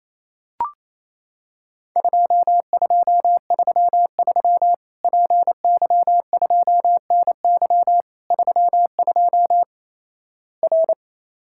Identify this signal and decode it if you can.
Morse code, heard as a keyed tone, about 27 wpm, 700 Hz. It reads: 2233 PY2NY 32 R
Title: Worked All Europe DX Contest - QTC Exchange - 28wpm